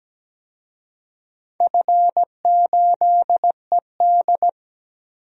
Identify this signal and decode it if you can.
Morse code, heard as a keyed tone, about 17 wpm, 700 Hz. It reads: F8ED
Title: DXCC-100 Call Signs - 4 Characters - 17wpm